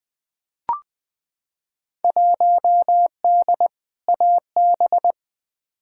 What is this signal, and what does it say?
Morse code, heard as a keyed tone, about 20 wpm, 700 Hz. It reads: 1D AB
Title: Field Day Exchanges Encore - 20wpm